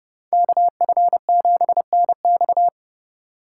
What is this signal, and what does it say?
Morse code, heard as a keyed tone, about 28 wpm, 700 Hz. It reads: KF7NX